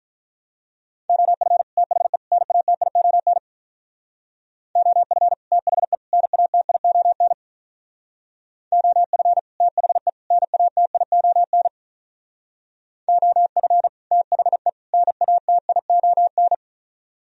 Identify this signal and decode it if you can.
Morse code, heard as a keyed tone, about 45 wpm, 700 Hz. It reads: OF THE NATION OF THE NATION OF THE NATION OF THE NATION